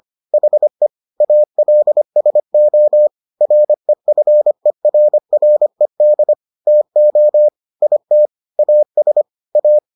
Morse code, heard as a keyed tone, about 25 wpm, 600 Hz. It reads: HE ALSO REFERRED TO IT AS A